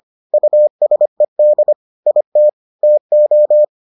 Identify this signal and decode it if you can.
Morse code, heard as a keyed tone, about 25 wpm, 600 Hz. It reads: USED IT TO